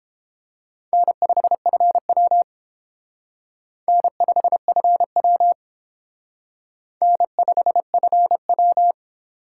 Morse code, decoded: N5FW N5FW N5FW

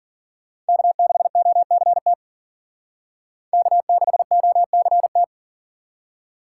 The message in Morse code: K6OCT K6OCT

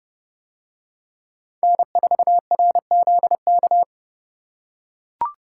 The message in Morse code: N4RZK

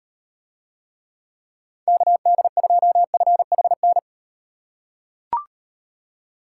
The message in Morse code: KD2FHN

